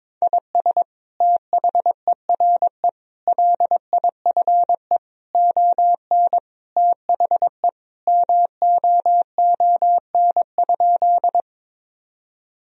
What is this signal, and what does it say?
Morse code, heard as a keyed tone, about 22 wpm, 700 Hz. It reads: IS THERE LIFE ON THE MOON?
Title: Sentences from Top 1000 Words - Review Part 3 - 22wpm